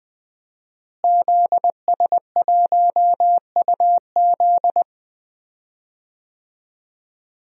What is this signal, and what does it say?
Morse code, heard as a keyed tone, about 20 wpm, 700 Hz. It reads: ZS1UZ
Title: DXCC-100 Call Signs - 5 Characters - Repeated 2x - 20wpm